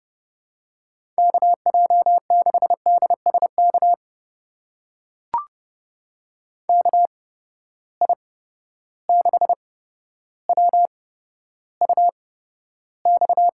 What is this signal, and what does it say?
Morse code, heard as a keyed tone, about 30 wpm, spaced as for 13 wpm, 700 Hz. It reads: KJ6DSK KI6WUX
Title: USA Call Signs - 6 Characters - 8x Character Spacing - 30wpm